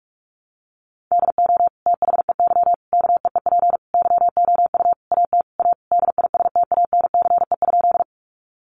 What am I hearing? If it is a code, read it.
Morse code, heard as a keyed tone, about 46 wpm, 700 Hz. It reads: DO THEY KEEP YOU AT A DISTANCE?
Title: Sentences from Top 1200 Words 45wpm